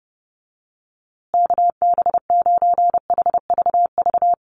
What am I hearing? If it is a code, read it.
Morse code, heard as a keyed tone, about 30 wpm, 700 Hz. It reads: KB9HVV
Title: USA Call Signs Repeated 3x 30wpm